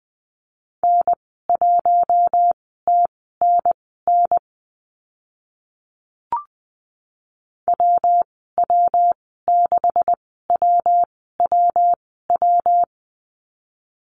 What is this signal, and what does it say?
Morse code, heard as a keyed tone, about 20 wpm, spaced as for 14 wpm, 700 Hz. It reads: N1TNN WW6WWW
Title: 140 Lesson 23g - Letter f - Call Signs 20wpm